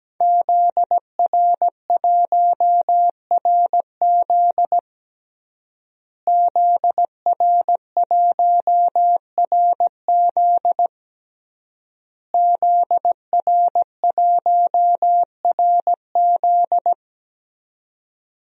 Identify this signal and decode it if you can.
Morse code, heard as a keyed tone, about 17 wpm, 700 Hz. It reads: ZR1RZ ZR1RZ ZR1RZ
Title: DXCC-100 Call Signs - 5 Characters - Repeated 3x - 17wpm